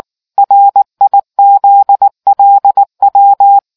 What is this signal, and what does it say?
Morse code, heard as a keyed tone, about 19 wpm, 800 Hz. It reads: RIZLW